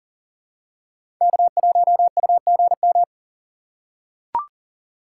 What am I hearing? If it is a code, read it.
Morse code, heard as a keyed tone, about 40 wpm, 700 Hz. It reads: K1UGM
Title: USA Call Signs - 5 Characters - 3x Character Spacing - 40wpm